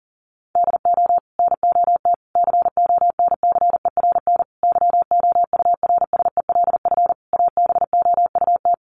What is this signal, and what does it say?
Morse code, heard as a keyed tone, about 40 wpm, 700 Hz. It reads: DO NOT CONCERN YOURSELF ABOUT
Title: Sentences from Top 1000 Words 40wpm